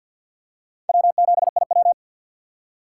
Morse code, heard as a keyed tone, about 51 wpm, 700 Hz. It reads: W7IW
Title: USA Call Signs - 3x Character Spacing - 50wpm